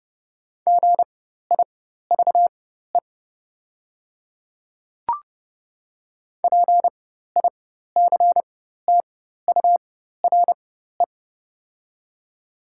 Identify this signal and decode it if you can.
Morse code, heard as a keyed tone, about 31 wpm, spaced as for 14 wpm, 700 Hz. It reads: GIVE PICTURE
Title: 4x Character Spacing - Top 200 Words - 30wpm